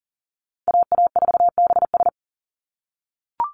Morse code, heard as a keyed tone, about 40 wpm, 700 Hz. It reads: AA4BS
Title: USA Call Signs 40wpm